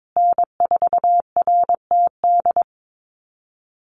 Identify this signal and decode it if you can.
Morse code, heard as a keyed tone, about 22 wpm, 700 Hz. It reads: N4RTD
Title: USA Call Signs Repeated 3x 22wpm